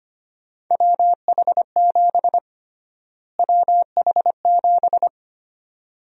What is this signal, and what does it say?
Morse code, heard as a keyed tone, about 25 wpm, 700 Hz. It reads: WH7 WH7
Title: USA Call Sign Prefixes - Repeated 2x - 25wpm